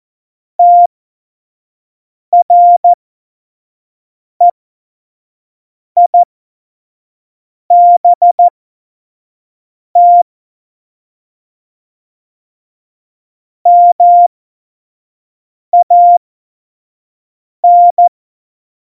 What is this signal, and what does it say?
Morse code, heard as a keyed tone, about 14 wpm, spaced as for 5 wpm, 700 Hz. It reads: TREIBT MAN
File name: Morsekurs / Witze / Witz 25 Juli 17, Tempo 25 F70